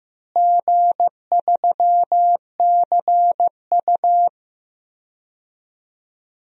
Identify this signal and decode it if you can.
Morse code, heard as a keyed tone, about 16 wpm, 700 Hz. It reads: G3CU
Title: DXCC-100 Call Signs - 4 Characters - Repeated 3x - 15wpm